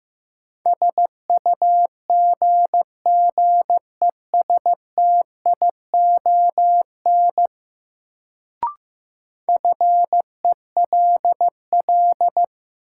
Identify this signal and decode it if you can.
Morse code, heard as a keyed tone, about 15 wpm, 700 Hz. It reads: SUGGESTION FELL
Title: Sentences from Top 1600 Words - New Word Drill Alt - 15wpm